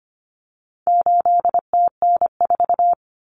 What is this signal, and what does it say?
Morse code, heard as a keyed tone, about 25 wpm, 700 Hz. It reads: 8TN4
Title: Common Cut Numbers - 4-Digit 25wpm